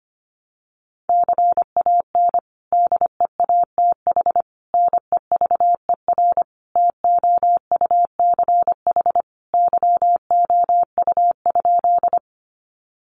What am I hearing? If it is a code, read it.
Morse code, heard as a keyed tone, about 25 wpm, 700 Hz. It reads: CAN DEATH NEVER TOUCH YOU?
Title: Sentences from Top 1200 Words 25wpm